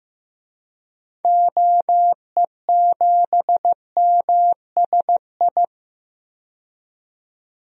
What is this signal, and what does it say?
Morse code, heard as a keyed tone, about 15 wpm, 700 Hz. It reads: OE7MSI
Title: DXCC-100 Call Signs - 6 Characters - 15wpm